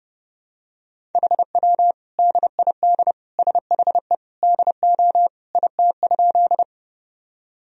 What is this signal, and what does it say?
Morse code, heard as a keyed tone, about 30 wpm, 700 Hz. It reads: HW DID SHE DO IT?
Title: Sentences from Top 100 Words and Abbreviations 30wpm